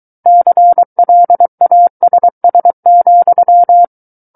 Morse code, heard as a keyed tone, about 23 wpm, 700 Hz. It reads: CLASS,
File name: Book One / 14-Chapter 14 - 23 WPM